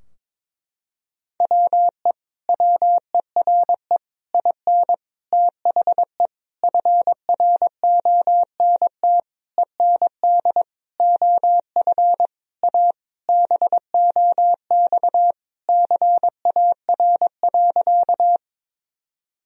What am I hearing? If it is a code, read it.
Morse code, heard as a keyed tone, about 22 wpm, 700 Hz. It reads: WE WERE IN THE FRONT END OF A BOX CAR.
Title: Sentences from Top 1000 Words - Review Part 2 - 22wpm